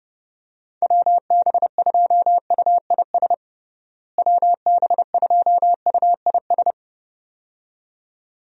Morse code, heard as a keyed tone, about 29 wpm, 700 Hz. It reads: WB2UIS WB2UIS
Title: USA Call Signs - 6 Characters - Repeated 2x - 30wpm